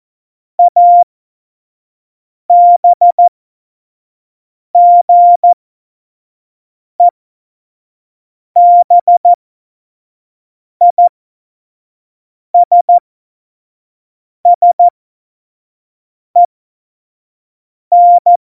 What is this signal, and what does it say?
Morse code, heard as a keyed tone, about 14 wpm, spaced as for 5 wpm, 700 Hz. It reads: ABGEBISSEN